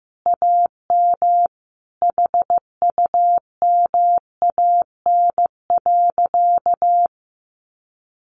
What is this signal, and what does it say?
Morse code, heard as a keyed tone, about 15 wpm, 700 Hz. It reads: AM HUMAN.
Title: Sentences from Top 600 Words 15wpm